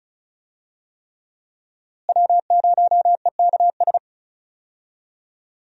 Morse code, heard as a keyed tone, about 34 wpm, 700 Hz. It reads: W0EKS
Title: USA Call Signs - 5 Characters - Repeated 3x - 35wpm